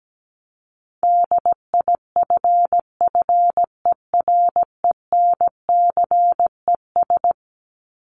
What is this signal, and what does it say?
Morse code, heard as a keyed tone, about 17 wpm, 700 Hz. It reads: DIFFERENCES
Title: ES:S Suffix - Top 100 Words with Base Words 17wpm